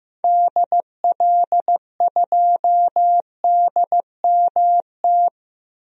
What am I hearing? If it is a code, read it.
Morse code, heard as a keyed tone, about 15 wpm, 700 Hz. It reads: DL2DMT
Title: DXCC-100 Call Signs - 6 Characters - Repeated 2x - 15wpm